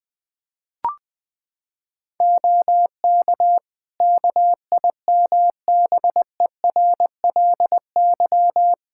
Morse code, heard as a keyed tone, about 20 wpm, 700 Hz. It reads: OK KIMBERLY
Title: Practice QSO - Exchange 2 - Info - 20wpm